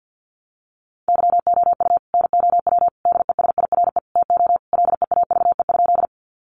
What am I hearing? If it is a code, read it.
Morse code, heard as a keyed tone, about 50 wpm, 700 Hz. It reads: YOU NOW DESIRE TO LEAVE?
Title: Sentences from Top 1100 Words 50wpm